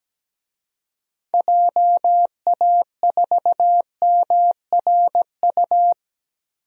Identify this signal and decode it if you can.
Morse code, heard as a keyed tone, about 17 wpm, 700 Hz. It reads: JA4MRU
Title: DXCC-100 Call Signs - 6 Characters - Repeated 2x - 17wpm